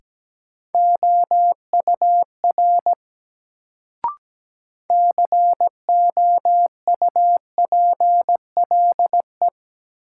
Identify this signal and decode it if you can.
Morse code, heard as a keyed tone, about 17 wpm, 700 Hz. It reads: OUR COUPLE
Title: Sets of 2 Words - New Word Drill - 17wpm